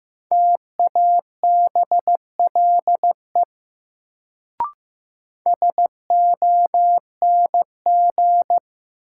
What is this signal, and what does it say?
Morse code, heard as a keyed tone, about 15 wpm, 700 Hz. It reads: TABLE SONG
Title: Sentences from Top 400 Words - New Word Drill - 15wpm